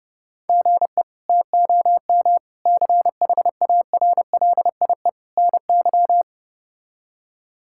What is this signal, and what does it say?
Morse code, heard as a keyed tone, about 32 wpm, 700 Hz. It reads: GE TOM CHARLIE NY
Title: K1USN SST QSO Practice - 30wpm